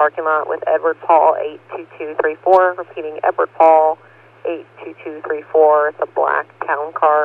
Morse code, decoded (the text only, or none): none